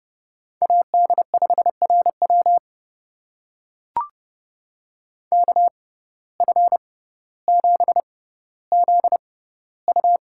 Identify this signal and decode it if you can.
Morse code, heard as a keyed tone, about 30 wpm, spaced as for 15 wpm, 700 Hz. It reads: AD5RW KF7ZU
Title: USA Call Signs - 5 Characters - 6x Character Spacing - 30wpm